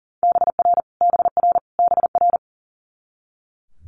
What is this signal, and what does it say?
Morse code, heard as a keyed tone, about 37 wpm, 700 Hz. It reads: BR BR BR
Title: Multi-letter Phonemes Repeated 3x 40wpm